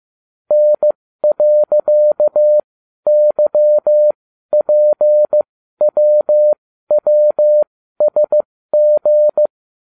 Morse code, heard as a keyed tone, about 15 wpm, spaced as for 12 wpm, 600 Hz. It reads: N.YPWWSG